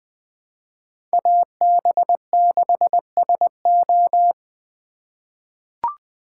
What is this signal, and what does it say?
Morse code, heard as a keyed tone, about 20 wpm, 700 Hz. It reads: AB6SO
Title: 2023 SOTA Chaser Call Signs - 20wpm